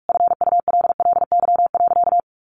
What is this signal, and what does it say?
Morse code, heard as a keyed tone, about 44 wpm, 700 Hz. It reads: FULLY.
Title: Sentences from Top 1300 Words 45wpm